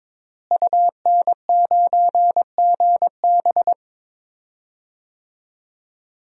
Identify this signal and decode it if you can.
Morse code, heard as a keyed tone, about 22 wpm, 700 Hz. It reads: UN9GB
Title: DXCC-100 Call Signs - 5 Characters - Repeated 2x - 22wpm